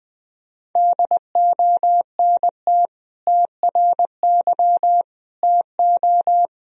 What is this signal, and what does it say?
Morse code, heard as a keyed tone, about 20 wpm, 700 Hz. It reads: DONT TRY TO
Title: Sentences from Top 1000 Words - Review Part 1 - 20wpm